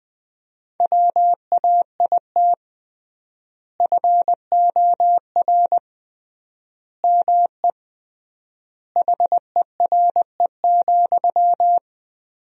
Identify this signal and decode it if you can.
Morse code, heard as a keyed tone, about 21 wpm, 700 Hz. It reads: WAIT FOR ME HERE,
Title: 3x Word-Spacing - Sentences from Top 500 Words - 20wpm